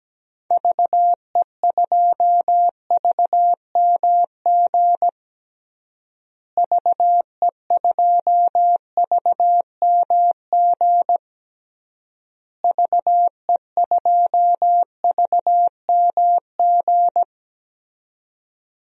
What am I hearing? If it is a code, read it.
Morse code, heard as a keyed tone, about 17 wpm, 700 Hz. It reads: VE2VMG VE2VMG VE2VMG
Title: DXCC-100 Call Signs - 6 Characters - Repeated 3x - 17wpm